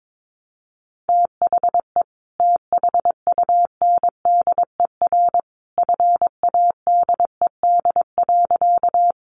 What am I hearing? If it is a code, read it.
Morse code, heard as a keyed tone, about 22 wpm, 700 Hz. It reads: THE THUNDER FADED.
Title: A Sound Of Thunder 22wpm